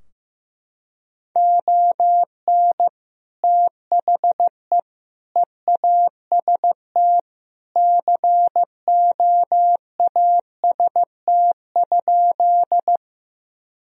Morse code, heard as a keyed tone, about 15 wpm, 700 Hz. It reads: ON THE EAST COAST?
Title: Sentences from Top 1000 Words - Review Part 4 - 15wpm